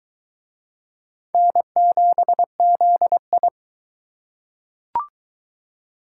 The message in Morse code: N7ZI

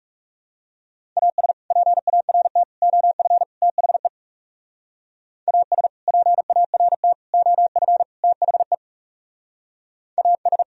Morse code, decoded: AS PART OF THE AS PART OF THE AS

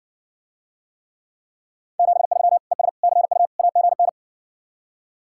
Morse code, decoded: 73 ES CU AGN

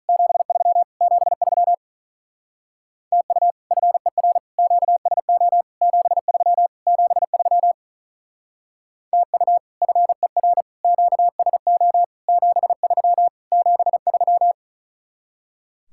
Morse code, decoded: 73 73 TU FER QSO 73 73 TU FER QSO 73 73

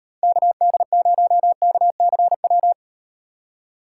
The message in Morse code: KD0KCW

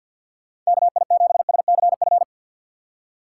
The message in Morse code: KI7SCF

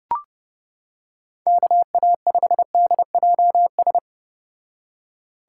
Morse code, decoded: KA5DJS

